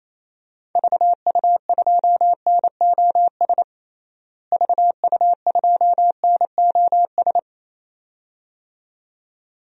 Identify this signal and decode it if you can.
Morse code, heard as a keyed tone, about 28 wpm, 700 Hz. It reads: VU2NOS VU2NOS
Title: DXCC-100 Call Signs - 6 Characters - Repeated 2x - 28wpm